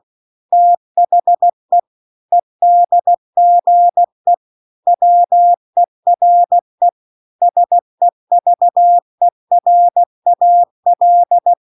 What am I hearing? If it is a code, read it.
Morse code, heard as a keyed tone, about 16 wpm, 700 Hz. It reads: THE EDGE WERE SEVERAL
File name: 16WPM